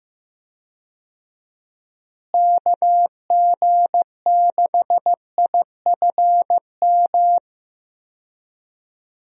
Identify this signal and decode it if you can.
Morse code, heard as a keyed tone, about 15 wpm, 700 Hz. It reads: KG6IFM
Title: USA Call Signs - 6 Characters - Repeated 2x - 15wpm